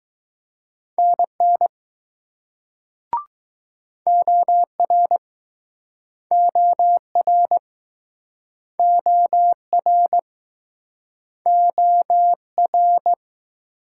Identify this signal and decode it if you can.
Morse code, heard as a keyed tone, about 21 wpm, 700 Hz. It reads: NN OR OR OR OR